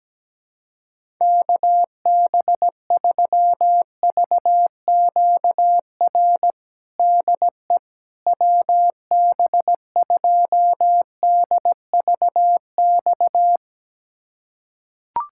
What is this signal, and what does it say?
Morse code, heard as a keyed tone, about 17 wpm, 700 Hz. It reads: KB3VQR DE WB2DVX
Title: USA Callsign DE Callsign - 6 Characters - 17wpm